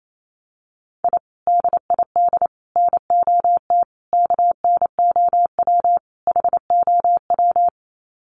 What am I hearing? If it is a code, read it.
Morse code, heard as a keyed tone, about 28 wpm, 700 Hz. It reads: I DID NOT KNOW HOW
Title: Sets of 5 Words Encore 28wpm